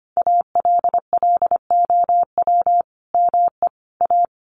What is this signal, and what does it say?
Morse code, heard as a keyed tone, about 25 wpm, 700 Hz. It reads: ALLOW ME A
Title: Sentences from Top 1000 Words 25wpm